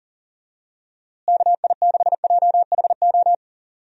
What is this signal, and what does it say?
Morse code, heard as a keyed tone, about 40 wpm, 700 Hz. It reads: KI6JHO
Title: USA Call Signs - 6 Characters - Repeated 2x - 40wpm